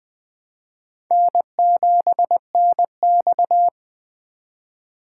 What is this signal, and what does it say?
Morse code, heard as a keyed tone, about 20 wpm, 700 Hz. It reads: N7NX
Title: USA Call Signs - 4 Characters - 3x Character Spacing - 20wpm